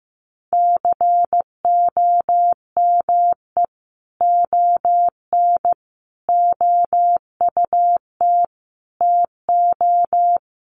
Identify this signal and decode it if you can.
Morse code, heard as a keyed tone, about 15 wpm, 700 Hz. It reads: COME ON OUT TO
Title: Sentences from Top 1500 Words 15wpm